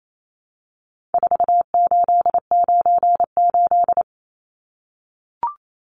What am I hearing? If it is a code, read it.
Morse code, heard as a keyed tone, about 28 wpm, 700 Hz. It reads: 4898
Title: Numbers - 4-Digit 28wpm